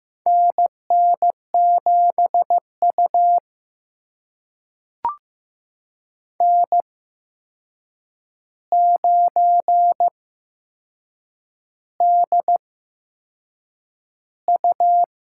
Morse code, heard as a keyed tone, about 15 wpm, spaced as for 6 wpm, 700 Hz. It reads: NN7U N9DU
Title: USA Call Signs - 4 Characters - 8x Character Spacing - 15wpm